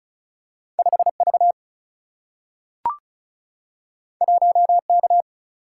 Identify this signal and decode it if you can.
Morse code, heard as a keyed tone, about 36 wpm, 700 Hz. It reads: 5 V 1 K
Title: Single-Letter Single-Number 35wpm